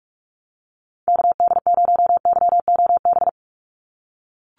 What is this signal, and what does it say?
Morse code, heard as a keyed tone, about 44 wpm, 700 Hz. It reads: KD0YOB
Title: NA SOTA Activator Callsigns 45wpm